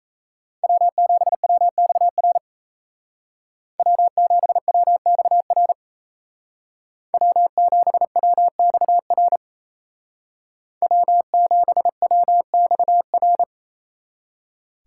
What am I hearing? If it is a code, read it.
Morse code, heard as a keyed tone, about 36 wpm, 700 Hz. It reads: W7WXR W7WXR W7WXR W7WXR